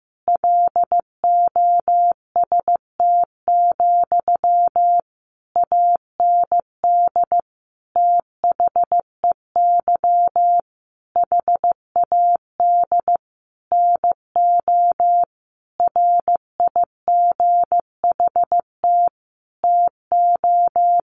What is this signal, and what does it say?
Morse code, heard as a keyed tone, about 15 wpm, 700 Hz. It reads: LOST, AND THEY HAD NO RIGHT TO
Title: Sentences from Top 1500 Words 15wpm